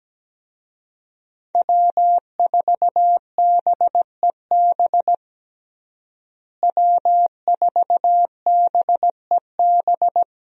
Morse code, heard as a keyed tone, about 17 wpm, 700 Hz. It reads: W4BEB W4BEB
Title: USA Call Signs - 5 Characters - Repeated 2x - 17wpm